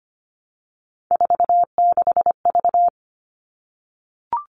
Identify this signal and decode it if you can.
Morse code, heard as a keyed tone, about 25 wpm, 700 Hz. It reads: 46V
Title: All Cut Numbers - 3-Digit 25wpm